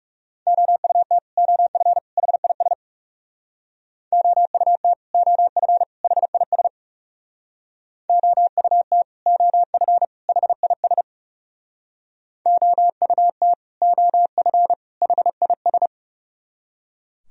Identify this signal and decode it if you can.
Morse code, heard as a keyed tone, about 37 wpm, 700 Hz. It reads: OUT OF HIS OUT OF HIS OUT OF HIS OUT OF HIS
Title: Speed-Racing - Sets of 3 Words 30wpm